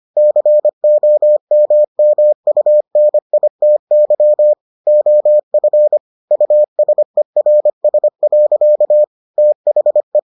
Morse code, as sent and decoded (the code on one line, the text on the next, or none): -.-. --- -- -- ..- -. .. - -.-- / --- ..-. / ..- ... .-. ... .-.-.- / - .... .
COMMUNITY OF USERS. THE